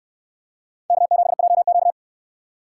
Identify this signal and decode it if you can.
Morse code, heard as a keyed tone, about 68 wpm, 700 Hz. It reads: K7JQ